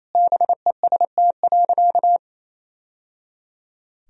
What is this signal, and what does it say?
Morse code, heard as a keyed tone, about 28 wpm, 700 Hz. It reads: BEST.
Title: Sentences from Top 1000 Words - Review Part 3 - 28wpm